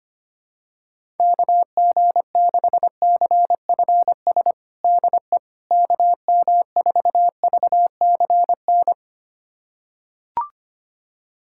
Morse code, heard as a keyed tone, about 25 wpm, 700 Hz. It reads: KG6CFS DE KM4VCN
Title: USA Callsign DE Callsign - 6 Characters - 25wpm